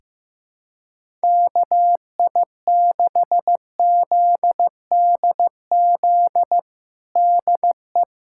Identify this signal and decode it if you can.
Morse code, heard as a keyed tone, about 16 wpm, 700 Hz. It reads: KI6ZDZ DE
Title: USA Callsign DE Callsign 15wpm